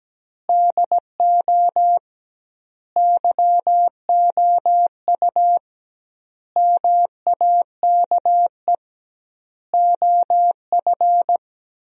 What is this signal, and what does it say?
Morse code, heard as a keyed tone, about 17 wpm, 700 Hz. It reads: DO YOU MAKE OF
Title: 2x Word-Spacing - Sentences from Top 100 Words - 17wpm